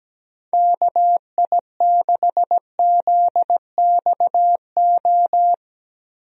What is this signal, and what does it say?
Morse code, heard as a keyed tone, about 17 wpm, 700 Hz. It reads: KI6ZXO
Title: USA Call Signs - 6 Characters - Repeated 3x - 17wpm